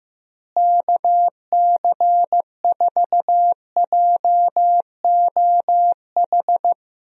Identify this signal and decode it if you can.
Morse code, heard as a keyed tone, about 15 wpm, 700 Hz. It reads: KC4JOH
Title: USA Call Signs - 6 Characters - Repeated 2x - 15wpm